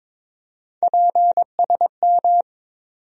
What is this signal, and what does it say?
Morse code, heard as a keyed tone, about 22 wpm, 700 Hz. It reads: PSM
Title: USA Call Sign Postfixes - 3 Letters 22wpm